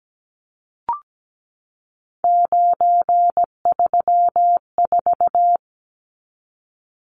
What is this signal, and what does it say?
Morse code, heard as a keyed tone, about 17 wpm, 700 Hz. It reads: 934
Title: Numbers - 3-Digit 17wpm